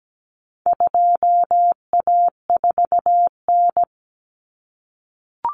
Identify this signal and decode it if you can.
Morse code, heard as a keyed tone, about 17 wpm, 700 Hz. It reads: 2A4N
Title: Common Cut Numbers - 4-Digit 17wpm